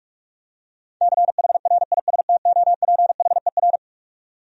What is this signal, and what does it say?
Morse code, heard as a keyed tone, about 47 wpm, 700 Hz. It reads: CHRISTOPHER
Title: USA Common Names - 45wpm